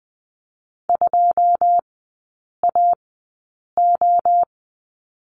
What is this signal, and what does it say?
Morse code, heard as a keyed tone, about 20 wpm, spaced as for 8 wpm, 690 Hz. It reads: 2AO
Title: 093 Lesson 16e - Letter c - Three Characters 20wpm